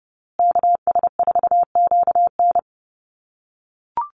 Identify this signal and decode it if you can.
Morse code, heard as a keyed tone, about 30 wpm, 700 Hz. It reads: KS4QN
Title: USA Call Signs Repeated 3x 30wpm